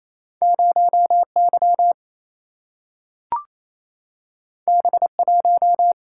Morse code, heard as a keyed tone, about 28 wpm, 700 Hz. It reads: 0Y B1
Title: Single-Letter Single-Number 28wpm